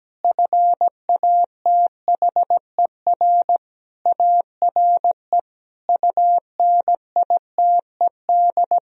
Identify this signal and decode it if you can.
Morse code, heard as a keyed tone, about 17 wpm, 700 Hz. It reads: FATHER ARE UNITED
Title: Sentences from Top 1000 Words - Review Part 4 - 17wpm